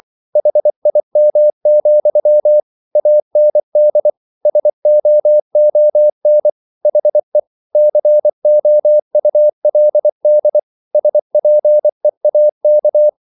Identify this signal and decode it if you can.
Morse code, heard as a keyed tone, about 24 wpm, 600 Hz. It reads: HIM, AND SOON HE COULD SPEAK